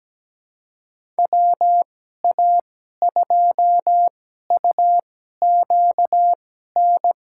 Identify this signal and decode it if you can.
Morse code, heard as a keyed tone, about 17 wpm, spaced as for 12 wpm, 700 Hz. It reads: WA2UQN
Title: USA Call Signs - 6 Characters - 2x Character Spacing - 17wpm